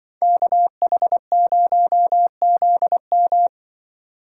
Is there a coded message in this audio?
Yes: Morse code from 0.2 to 3.5 s, heard as a keyed tone, about 25 wpm, 700 Hz. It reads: KH0ZM